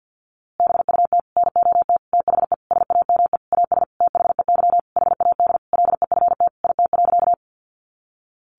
Morse code, heard as a keyed tone, about 50 wpm, 700 Hz. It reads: BUT NOT THE SAME AS THEY HAD LEFT IT.
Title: A Sound Of Thunder 50wpm